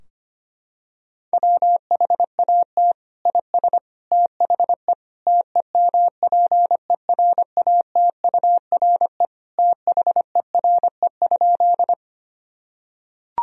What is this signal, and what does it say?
Morse code, heard as a keyed tone, about 25 wpm, 700 Hz. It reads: WHAT IS THE TEMPERATURE THERE?
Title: Sentences from Top 1000 Words - Review Part 4 - 25wpm